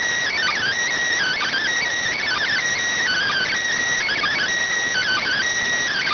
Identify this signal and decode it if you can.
Emergency Alert System (SAME) tones (header bursts), heard as harsh buzzing bursts. It is heard from 1.1 to 5.5 s.